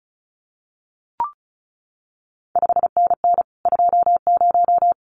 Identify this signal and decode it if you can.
Morse code, heard as a keyed tone, about 36 wpm, 700 Hz. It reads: 5NN 20
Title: CQ WW DX Contest Exchange - 35wpm